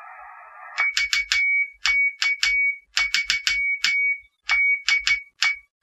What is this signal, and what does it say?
Morse code, heard as a keyed tone, about 14 wpm, 2250 Hz. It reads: VK3DE